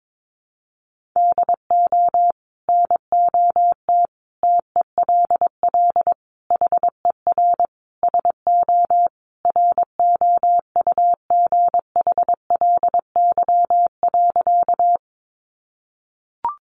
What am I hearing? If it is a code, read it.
Morse code, heard as a keyed tone, about 22 wpm, 700 Hz. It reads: DO NOT TELL HER SO ROUGHLY.
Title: Sentences from Top 1500 Words 22wpm